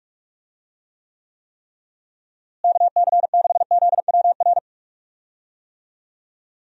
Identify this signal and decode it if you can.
Morse code, heard as a keyed tone, about 46 wpm, 700 Hz. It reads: KC6ZWR
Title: USA Call Signs - 6 Characters - Repeated 3x - 45wpm